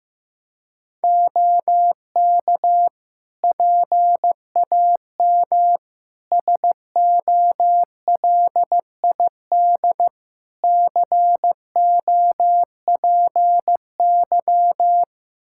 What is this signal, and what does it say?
Morse code, heard as a keyed tone, about 15 wpm, 700 Hz. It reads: OK PAM SOLID COPY
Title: K1USN SST QSO Practice 15wpm